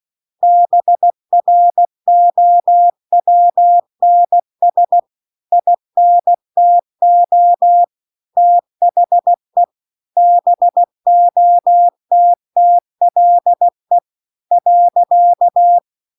Morse code, heard as a keyed tone, about 16 wpm, 700 Hz. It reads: BROWNS INTO THE BOTTLE .
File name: 16WPM